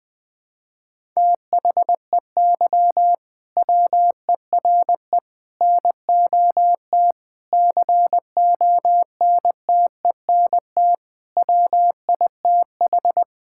Morse code, heard as a keyed tone, about 20 wpm, 700 Hz. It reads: THEY WERE NOT CONTENT WITH
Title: Sentences from Top 1000 Words - Review Part 6 - 20wpm